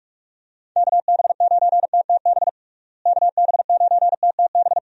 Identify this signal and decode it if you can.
Morse code, heard as a keyed tone, about 42 wpm, 700 Hz. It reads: KB9TTB KB9TTB
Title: USA Call Signs - 6 Characters - Repeated 2x - 45wpm